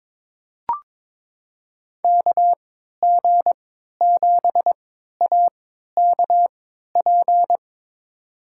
Morse code, heard as a keyed tone, about 22 wpm, spaced as for 12 wpm, 700 Hz. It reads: KG7AKP
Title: USA Call Signs - 3x Character Spacing - 22wpm